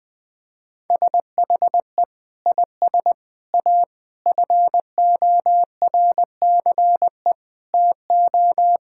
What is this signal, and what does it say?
Morse code, heard as a keyed tone, about 21 wpm, 700 Hz. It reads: SHE IS A FORCE TO
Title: Sentences with 6-Word Phrases - 20wpm